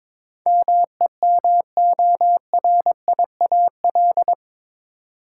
Morse code, heard as a keyed tone, about 21 wpm, 700 Hz. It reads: MEMORIAL
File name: practice / 22wpm / Eight Letter Words - 22wpm